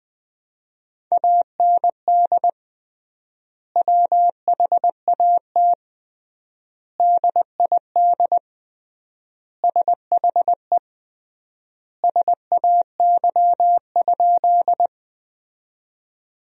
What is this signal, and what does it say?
Morse code, heard as a keyed tone, about 20 wpm, 700 Hz. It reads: AND WHAT DID SHE SAY?
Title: 3x Word-Spacing - Sentences from Top 200 Words - 20wpm